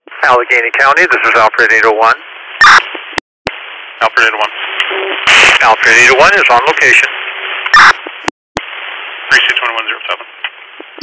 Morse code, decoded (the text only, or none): none